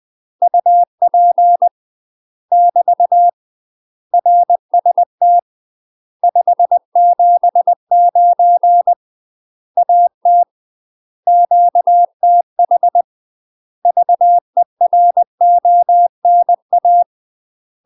Morse code, heard as a keyed tone, about 20 wpm, 700 Hz. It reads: UP <BT> RST 579 AT QTH VERONA
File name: UK MASTER QSOs TRAINING / 20WPM / UK MASTER QSOs 0031